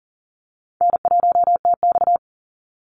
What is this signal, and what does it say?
Morse code, heard as a keyed tone, about 40 wpm, 700 Hz. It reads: N1TX
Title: Easy Call Signs - 40wpm